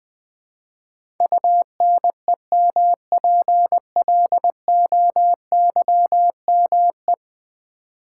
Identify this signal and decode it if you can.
Morse code, heard as a keyed tone, about 20 wpm, 700 Hz. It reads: UNEMPLOYME
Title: Build A Word - 5 Letters At A Time - 12-Letter Words - 20wpm